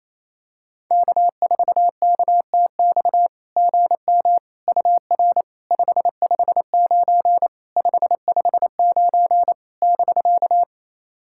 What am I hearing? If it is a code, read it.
Morse code, heard as a keyed tone, about 28 wpm, 700 Hz. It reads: K4KTX GM UR 559 559 <BK>
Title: SOTA QSO Practice 28wpm